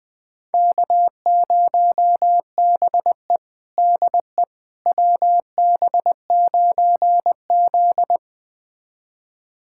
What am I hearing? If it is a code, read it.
Morse code, heard as a keyed tone, about 20 wpm, 700 Hz. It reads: K0BE DE WB9Z
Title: USA Callsign DE Callsign - 4 Characters - 20wpm